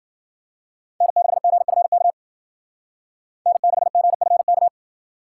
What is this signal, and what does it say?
Morse code, heard as a keyed tone, about 50 wpm, 700 Hz. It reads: N6GFK N6GFK
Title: Speed Racing - USA 1x3 Call Signs 40wpm